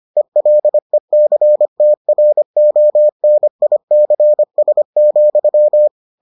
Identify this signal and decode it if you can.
Morse code, heard as a keyed tone, about 25 wpm, 600 Hz. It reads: ELECTRONICS,